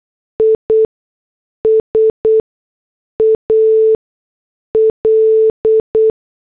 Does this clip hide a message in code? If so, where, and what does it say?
Yes: Morse code from 0.4 to 6.1 s, heard as a keyed tone, about 8 wpm, spaced as for 6 wpm, 440 Hz. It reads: IS AL